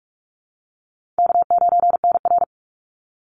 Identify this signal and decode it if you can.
Morse code, heard as a keyed tone, about 44 wpm, 700 Hz. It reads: K9NR
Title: Easy Call Signs Repeated 3x - 45wpm